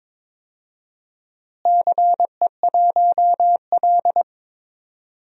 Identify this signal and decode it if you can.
Morse code, heard as a keyed tone, about 22 wpm, 700 Hz. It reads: CE1L
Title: DXCC-100 Call Signs - 4 Characters - Repeated 3x - 22wpm